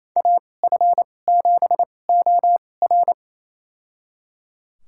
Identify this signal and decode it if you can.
Morse code, heard as a keyed tone, about 28 wpm, spaced as for 20 wpm, 700 Hz. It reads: AF7OR
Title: USA Call Signs - 5 Characters - 2x Character Spacing - 28wpm